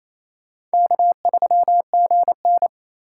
Morse code, heard as a keyed tone, about 35 wpm, 700 Hz. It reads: K3GN
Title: Speed Racing - USA 1x2 Call Signs 28wpm